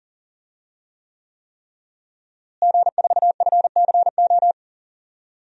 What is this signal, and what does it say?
Morse code, heard as a keyed tone, about 40 wpm, 700 Hz. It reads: G4FCO
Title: DXCC-100 Call Signs - 5 Characters - Repeated 3x - 40wpm